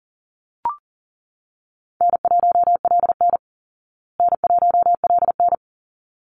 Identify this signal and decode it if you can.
Morse code, heard as a keyed tone, about 40 wpm, 700 Hz. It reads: N1LN N1LN
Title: Call Signs Repeated 3x - 40wpm